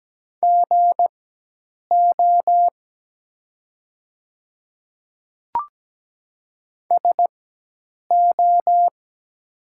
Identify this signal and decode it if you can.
Morse code, heard as a keyed tone, about 17 wpm, spaced as for 8 wpm, 700 Hz. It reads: GO SO
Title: 4x Character Spacing - Top 100 Words - 17wpm